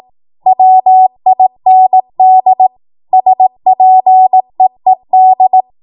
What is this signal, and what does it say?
Morse code, heard as a keyed tone, about 18 wpm, 750 Hz. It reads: WIND SPEED
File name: Morse / 18WPM